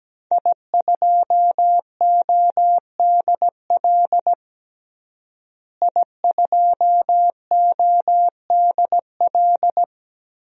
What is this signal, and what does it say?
Morse code, heard as a keyed tone, about 17 wpm, 700 Hz. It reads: I2ODL I2ODL
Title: DXCC-100 Call Signs - 5 Characters - Repeated 3x - 17wpm